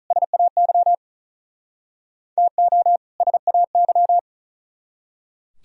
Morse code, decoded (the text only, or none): SAY TO SAY